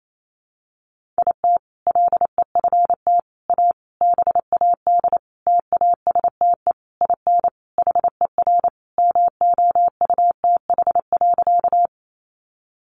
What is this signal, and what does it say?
Morse code, heard as a keyed tone, about 28 wpm, 700 Hz. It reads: IT LEFT A BAD TASTE IN HER MOUTH.
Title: Sentences from Top 1500 Words 28wpm